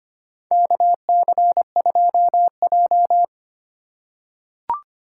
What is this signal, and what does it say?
Morse code, heard as a keyed tone, about 25 wpm, 700 Hz. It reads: KC2J
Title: USA Call Signs - 3x Character Spacing - 25wpm